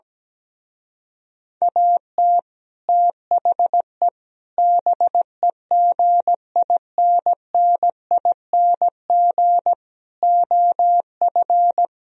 Morse code, heard as a keyed tone, about 17 wpm, 700 Hz. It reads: AT THE BEGINNING OF
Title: Speed-Racing - Sets of 4 Words 17wpm